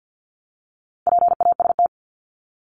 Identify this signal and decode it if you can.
Morse code, heard as a keyed tone, about 50 wpm, 700 Hz. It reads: PAST
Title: Top 501-750 Words 50wpm